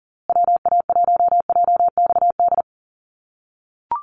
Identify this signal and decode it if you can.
Morse code, heard as a keyed tone, about 40 wpm, 700 Hz. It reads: WA1JXD
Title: USA Call Signs Repeated 2x 40wpm